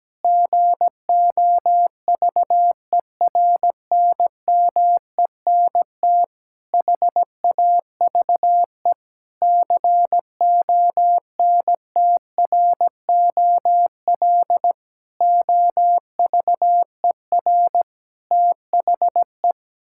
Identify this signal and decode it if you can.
Morse code, heard as a keyed tone, about 17 wpm, 700 Hz. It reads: GOVERNMENT HAVE CONTROL OVER THE
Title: Sentences from Top 1000 Words - Review Part 5 - 17wpm